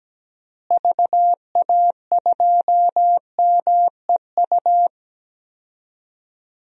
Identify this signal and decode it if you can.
Morse code, heard as a keyed tone, about 17 wpm, 700 Hz. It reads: VA2MEU
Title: DXCC-100 Call Signs - 6 Characters - Repeated 3x - 17wpm